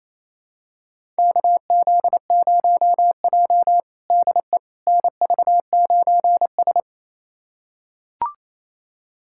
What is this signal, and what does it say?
Morse code, heard as a keyed tone, about 28 wpm, 700 Hz. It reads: KZ0J DE NV9S
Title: USA Callsign DE Callsign - 4 Characters - 28wpm